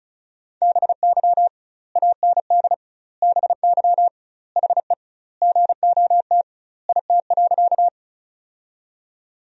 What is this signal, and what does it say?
Morse code, heard as a keyed tone, about 36 wpm, 700 Hz. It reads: BY AND BY HE GOT IT.
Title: 2x Word-Spacing - Sentences from Top 300 Words - 35wpm